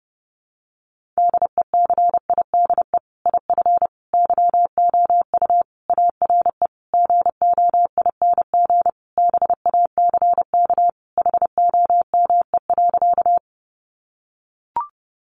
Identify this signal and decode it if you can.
Morse code, heard as a keyed tone, about 30 wpm, 700 Hz. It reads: DECIDE IF YOU ARE GOING BACK HOME.
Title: Sentences from Top 500 Words 30wpm